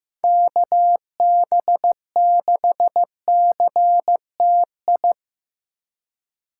Morse code, heard as a keyed tone, about 17 wpm, 700 Hz. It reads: KB6CTI
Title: Speed Racing - USA 2x3 Call Signs 15wpm